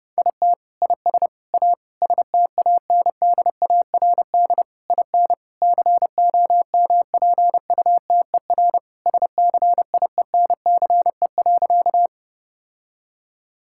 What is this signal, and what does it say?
Morse code, heard as a keyed tone, about 30 wpm, 700 Hz. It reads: IT IS A STANDARD IN COMPUTER SCIENCE.
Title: Sentences from Top 1000 Words - Review Part 6 - 30wpm